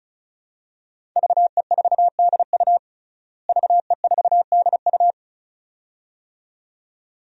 Morse code, heard as a keyed tone, about 36 wpm, 700 Hz. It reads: VE4DU VE4DU
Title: DXCC-100 Call Signs - 5 Characters - Repeated 2x - 35wpm